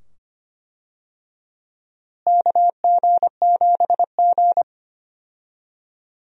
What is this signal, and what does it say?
Morse code, heard as a keyed tone, about 25 wpm, 700 Hz. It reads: KG7G
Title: USA Call Signs - 4 Characters - Repeated 2x - 25wpm